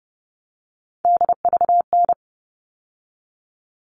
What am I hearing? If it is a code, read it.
Morse code, heard as a keyed tone, about 30 wpm, 700 Hz. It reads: DVN
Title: All Cut Numbers - 3-Digit 30wpm